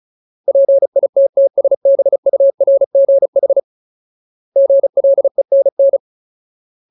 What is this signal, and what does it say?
Morse code, heard as a keyed tone, about 34 wpm, 550 Hz. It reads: PITTSBURGH GLENN
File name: Ultimate Word List / 35WPM Ultimate